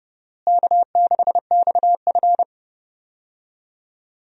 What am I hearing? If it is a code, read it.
Morse code, heard as a keyed tone, about 31 wpm, 700 Hz. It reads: K6XF
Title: USA Call Signs - 4 Characters - Repeated 2x - 30wpm